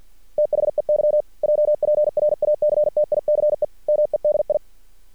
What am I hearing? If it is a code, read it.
Morse code, heard as a keyed tone, about 50 wpm, 600 Hz. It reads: THEY JPRACTICE MEDI
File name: archived / CRSnet